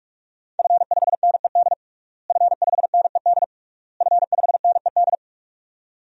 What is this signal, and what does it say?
Morse code, heard as a keyed tone, about 46 wpm, 700 Hz. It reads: F5NED F5NED F5NED
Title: DXCC-100 Call Signs - 5 Characters - Repeated 3x - 45wpm